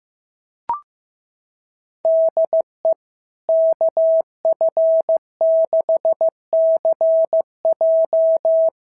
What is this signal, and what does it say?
Morse code, heard as a keyed tone, about 15 wpm, 660 Hz. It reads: DE KF6CJ
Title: SOTA QSO Practice 15wpm